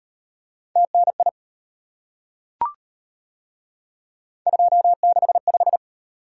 Morse code, heard as a keyed tone, about 38 wpm, 700 Hz. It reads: TNI 265